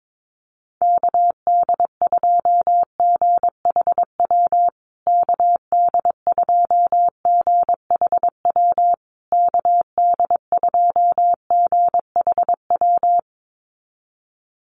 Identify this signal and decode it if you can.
Morse code, heard as a keyed tone, about 22 wpm, 700 Hz. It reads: KD2GHW KD2GHW KD2GHW
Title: USA Call Signs Repeated 3x 22wpm